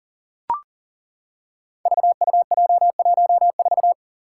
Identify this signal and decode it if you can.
Morse code, heard as a keyed tone, about 38 wpm, 700 Hz. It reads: VUJ14
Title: Five Challenging Letter-Number Combinations 40wpm